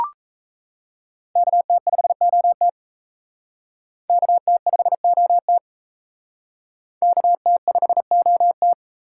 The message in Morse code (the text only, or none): KT5OT KT5OT KT5OT